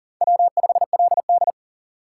Morse code, heard as a keyed tone, about 40 wpm, 700 Hz. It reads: W5LD
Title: USA Call Signs - 4 Characters - 3x Character Spacing - 40wpm